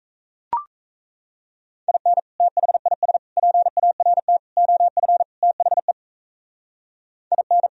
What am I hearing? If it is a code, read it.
Morse code, heard as a keyed tone, about 34 wpm, 700 Hz. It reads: IN THIS PART OF THE IN
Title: Speed-Racing - Sets of 5 Words 28wpm